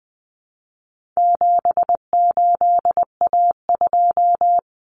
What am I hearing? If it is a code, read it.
Morse code, heard as a keyed tone, about 20 wpm, 700 Hz. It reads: 78A2
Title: Common Cut Numbers - 4-Digit 20wpm